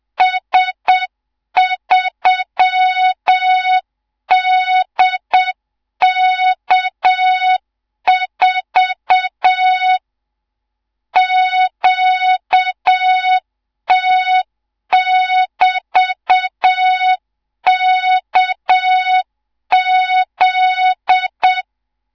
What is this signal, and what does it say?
Morse code, heard as a keyed tone, about 7 wpm, 3700 Hz, with noise. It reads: S3DK4 QT<BT>KZ
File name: Morze-HA2MN / hang / 18 IRASJELEK2